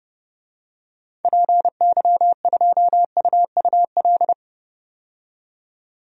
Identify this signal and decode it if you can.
Morse code, heard as a keyed tone, about 31 wpm, 700 Hz. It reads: PY2UUL
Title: DXCC-100 Call Signs - 6 Characters - 30wpm